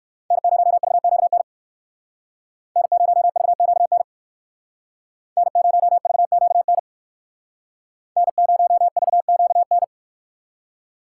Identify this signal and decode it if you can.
Morse code, heard as a keyed tone, about 56 wpm, 700 Hz. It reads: N0VQN N0VQN N0VQN N0VQN
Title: Speed Racing - USA 1x3 Call Signs 45wpm